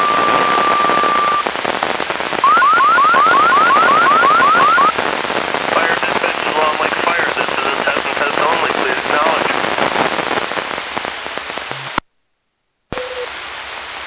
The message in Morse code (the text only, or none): none